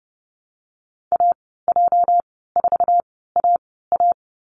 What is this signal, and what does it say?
Morse code, heard as a keyed tone, about 30 wpm, spaced as for 17 wpm, 700 Hz. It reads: AJ4AA
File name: Lesson 37g - Letter x - Call Signs 30wpm